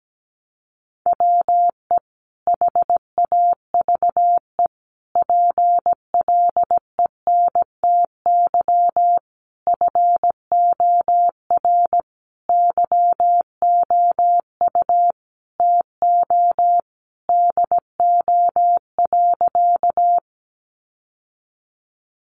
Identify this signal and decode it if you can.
Morse code, heard as a keyed tone, about 17 wpm, 700 Hz. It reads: WE HAVE PLENTY FOR YOU TO DO.
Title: Sentences from Top 1300 Words 17wpm